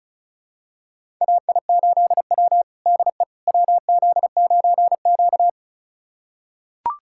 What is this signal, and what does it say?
Morse code, heard as a keyed tone, about 36 wpm, 700 Hz. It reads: AI8W DE WZ9Q
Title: USA Callsign DE Callsign - 4 Characters - 35wpm